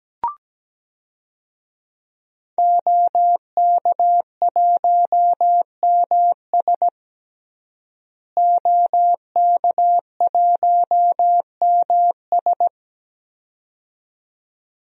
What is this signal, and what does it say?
Morse code, heard as a keyed tone, about 17 wpm, 700 Hz. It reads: OK1MS OK1MS
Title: DXCC-100 Call Signs - 5 Characters - Repeated 2x - 17wpm